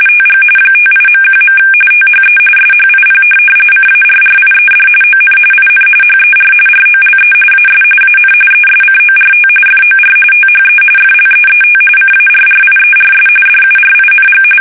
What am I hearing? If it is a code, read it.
RTTY, about 45.45 baud. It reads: TG 58.1. THE SHIP CARRIED ON IN HER ROLE AS AN ESCORT FOR THE FLEET CARRIERS DURING
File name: fsk850 / USS-Massachusetts